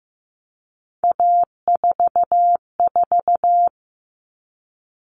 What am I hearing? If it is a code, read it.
Morse code, heard as a keyed tone, about 15 wpm, 700 Hz. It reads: A44
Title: Common Cut Numbers - 3-Digit 15wpm